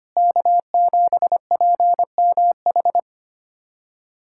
Morse code, heard as a keyed tone, about 25 wpm, 700 Hz. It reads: K7PMH
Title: USA Callsign DE Callsign - 5 Characters - 25wpm